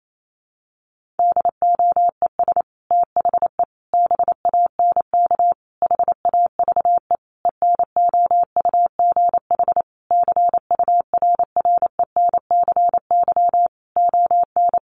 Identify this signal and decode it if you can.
Morse code, heard as a keyed tone, about 28 wpm, 700 Hz. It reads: DOES THE BANK HAVE ENOUGH CURRENCY ON
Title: Sentences from Top 1500 Words 28wpm